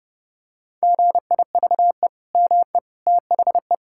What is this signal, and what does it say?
Morse code, heard as a keyed tone, about 30 wpm, 700 Hz. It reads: GIVE ME THE
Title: Sentences from Top 1000 Words - Review Part 6 - 30wpm